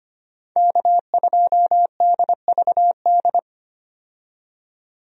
Morse code, heard as a keyed tone, about 25 wpm, 700 Hz. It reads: K2DVD
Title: USA Call Signs - 5 Characters - 25wpm